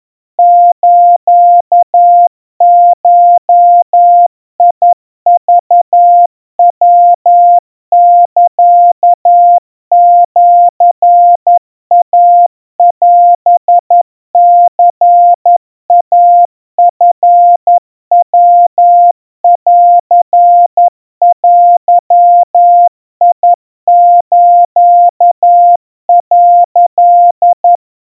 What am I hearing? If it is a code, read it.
Morse code (Japanese Wabun code), heard as a keyed tone, about 11 wpm, 700 Hz. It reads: スゴクヤサシイオニイチヤンデス」
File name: kawa / Morse sample / kamyu 40 cpm 700 Hz Crispy